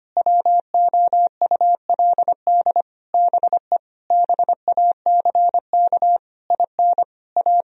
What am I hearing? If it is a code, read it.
Morse code, heard as a keyed tone, about 25 wpm, 700 Hz. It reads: WOULD BE BACK IN A
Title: Sentences from Top 1000 Words - Review Part 2 - 25wpm